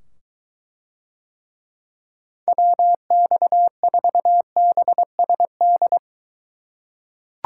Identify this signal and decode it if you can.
Morse code, heard as a keyed tone, about 23 wpm, 700 Hz. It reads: WX4BSD